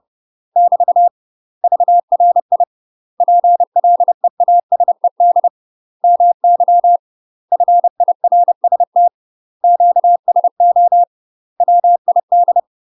Morse code, heard as a keyed tone, about 30 wpm, 700 Hz. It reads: <BT> VRI PLEASED MY FIRST QSO WID
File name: UK MASTER QSOs TRAINING/30WPM